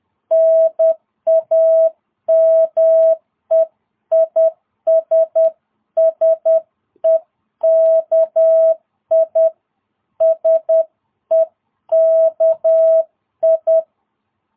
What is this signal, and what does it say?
Morse code, heard as a keyed tone, about 10 wpm, 650 Hz. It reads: NAME IS SEKI SEKI
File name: jh8cbh / 2022cw